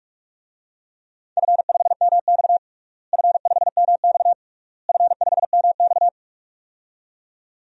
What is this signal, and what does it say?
Morse code, heard as a keyed tone, about 43 wpm, 700 Hz. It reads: F5MX F5MX F5MX
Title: DXCC-100 Call Signs - 4 Characters - Repeated 3x - 45wpm